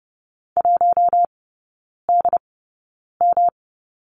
Morse code, heard as a keyed tone, about 30 wpm, spaced as for 9 wpm, 700 Hz. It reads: 1DM
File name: Lesson 18e - Letter m - Three Characters 30wpm